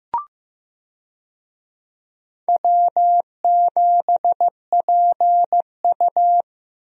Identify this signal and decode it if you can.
Morse code, heard as a keyed tone, about 15 wpm, 700 Hz. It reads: W7PU
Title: USA Call Signs - 4 Characters - Repeated 2x - 15wpm